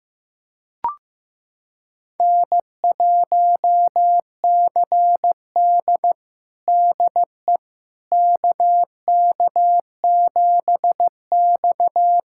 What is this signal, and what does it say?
Morse code, heard as a keyed tone, about 15 wpm, 700 Hz. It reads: N1CD DE KK7X
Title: USA Callsign DE Callsign - 4 Characters - 15wpm